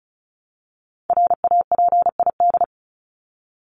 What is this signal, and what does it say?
Morse code, heard as a keyed tone, about 34 wpm, 700 Hz. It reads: RAPID
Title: LY Suffix - Top 100 Words with Base Words 35wpm